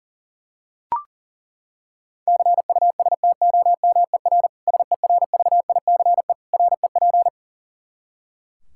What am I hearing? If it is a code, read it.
Morse code, heard as a keyed tone, about 40 wpm, 700 Hz. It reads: CUSTOMER SERVICE REP
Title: Practice QSO - Exchange 3 - More Info - 40wpm